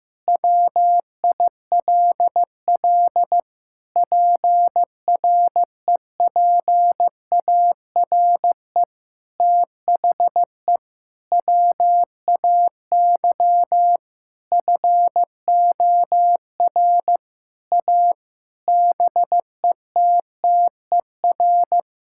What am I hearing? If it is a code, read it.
Morse code, heard as a keyed tone, about 15 wpm, 700 Hz. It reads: WILL PREPARE THE WAY FOR A BETTER
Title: Sentences from Top 1000 Words - Review Part 6 - 15wpm